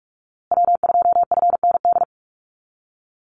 Morse code, heard as a keyed tone, about 44 wpm, 700 Hz. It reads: W2FND
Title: USA Call Signs Repeated 2x 45wpm